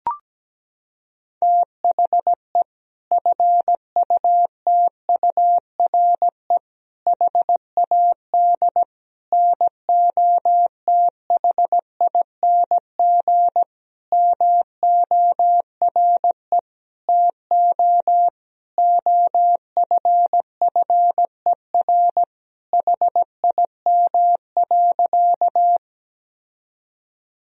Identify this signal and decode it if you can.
Morse code, heard as a keyed tone, about 17 wpm, 700 Hz. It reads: THE FUTURE HAD NOTHING MORE TO OFFER HIM.
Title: Sentences from Top 1000 Words - Review Part 4 - 17wpm